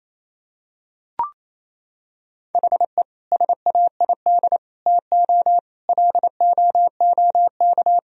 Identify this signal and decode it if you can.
Morse code, heard as a keyed tone, about 28 wpm, 700 Hz. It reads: HE SAID TO LOOK
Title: Sentences from Top 1000 Words - Review Part 1 - 28wpm